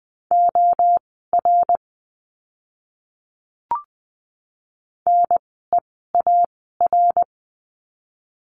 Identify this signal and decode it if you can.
Morse code, heard as a keyed tone, about 20 wpm, spaced as for 14 wpm, 700 Hz. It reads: OR NEAR
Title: 061 Lesson 11f - Letter h - Words 20wpm